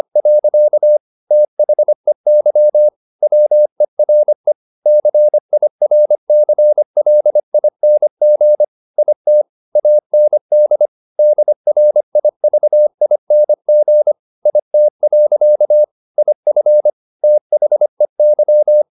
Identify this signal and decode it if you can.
Morse code, heard as a keyed tone, about 25 wpm, 600 Hz. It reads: . THEY WERE CIRCLING IT AND DRIVING IT. IF THEY
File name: the old man and the sea part 1 25wpm